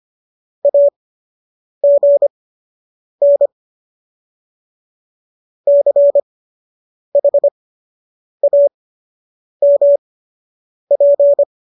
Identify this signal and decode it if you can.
Morse code, heard as a keyed tone, about 25 wpm, spaced as for 8 wpm, 570 Hz. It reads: AGN CHAMP